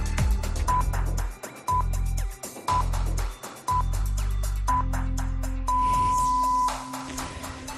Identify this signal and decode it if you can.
Radio time signal (time pips). It is heard from 0.7 to 6.7 s.